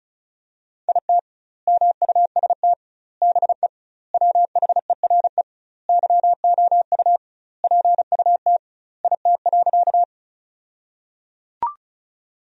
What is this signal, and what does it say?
Morse code, heard as a keyed tone, about 34 wpm, 700 Hz. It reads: IT MUST BE WHERE YOU PUT IT.
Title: 2x Word-Spacing - Sentences from Top 200 Words - 35wpm